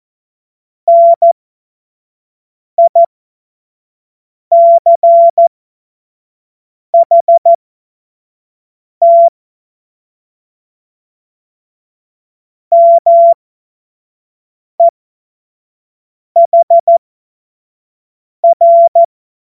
Morse code, heard as a keyed tone, about 14 wpm, spaced as for 5 wpm, 680 Hz. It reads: NICHT MEHR
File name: Morsekurs / Witze / Witz 19 März 2024 25 F70